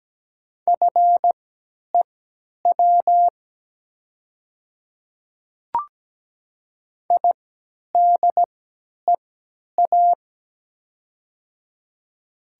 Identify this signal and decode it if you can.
Morse code, heard as a keyed tone, about 17 wpm, spaced as for 10 wpm, 700 Hz. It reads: FEW IDEA
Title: 3x Character Spacing - Top 500 Words - 17wpm